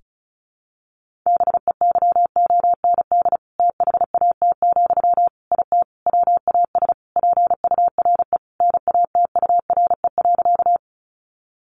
Morse code, heard as a keyed tone, about 34 wpm, 700 Hz. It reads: BEYOND THAT, IT WAS PURE NATURE.
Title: Sentences from Top 1500 Words 35wpm